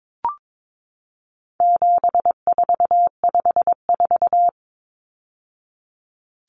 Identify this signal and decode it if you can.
Morse code, heard as a keyed tone, about 22 wpm, 700 Hz. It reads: 7454
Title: Numbers - 4-Digit 22wpm